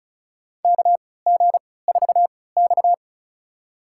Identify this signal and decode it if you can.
Morse code, heard as a keyed tone, about 36 wpm, spaced as for 20 wpm, 700 Hz. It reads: KG4X
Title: USA Call Signs - 4 Characters - 3x Character Spacing - 35wpm